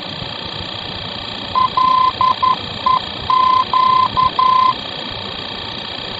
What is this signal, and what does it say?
Morse code, heard as a keyed tone, about 11 wpm, 1030 Hz, with radio noise. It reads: LEQ
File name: france / mx / LEQ-F-VOR